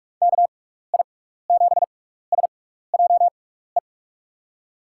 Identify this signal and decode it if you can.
Morse code, heard as a keyed tone, about 44 wpm, spaced as for 15 wpm, 700 Hz. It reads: KI7SJE
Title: USA Call Signs - 6x Character Spacing - 45wpm